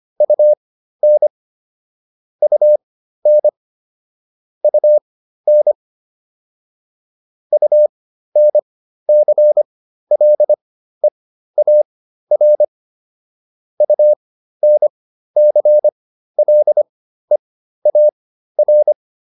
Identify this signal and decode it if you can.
Morse code, heard as a keyed tone, about 25 wpm, spaced as for 13 wpm, 600 Hz. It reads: UN UN UN UNCLEAR UNCLEAR
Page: intermediate-practice-files